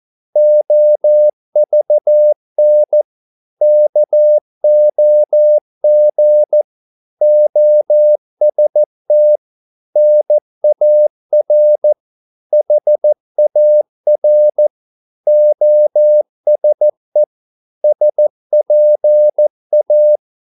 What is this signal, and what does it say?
Morse code, heard as a keyed tone, about 14 wpm, 600 Hz. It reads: OVN KOG OST NAR HAR OSE SPA
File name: Korte ord DK 14wpm